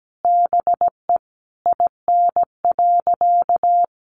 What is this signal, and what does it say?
Morse code, heard as a keyed tone, about 17 wpm, 700 Hz. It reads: BE IN.
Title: Sentences from Top 1200 Words 17wpm